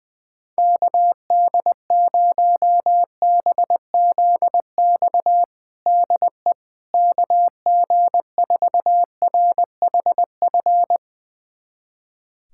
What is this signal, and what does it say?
Morse code, heard as a keyed tone, about 20 wpm, 700 Hz. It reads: KD0BZX DE KG4RHF
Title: USA Callsign DE Callsign - 6 Characters - 20wpm